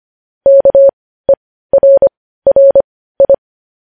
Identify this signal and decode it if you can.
Morse code, heard as a keyed tone, about 25 wpm, spaced as for 15 wpm, 550 Hz. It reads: KERRI
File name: podcast / News headlines-1x-25x15